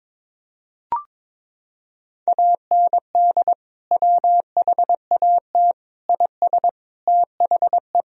Morse code, heard as a keyed tone, about 22 wpm, 700 Hz. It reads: AND WHAT IS THE
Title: Sentences from Top 1000 Words - Review Part 3 - 22wpm